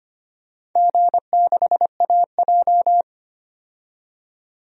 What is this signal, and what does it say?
Morse code, heard as a keyed tone, about 25 wpm, 700 Hz. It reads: G6AJ
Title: DXCC-100 Call Signs - 4 Characters - 25wpm